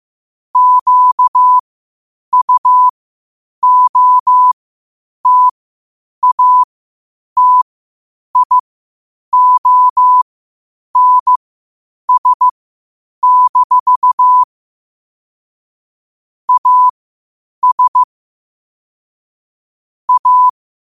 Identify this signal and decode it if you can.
Morse code, heard as a keyed tone, about 15 wpm, spaced as for 8 wpm, 1000 Hz. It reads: QUOTATIONS- AS A
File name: qotdfiles / 8wpm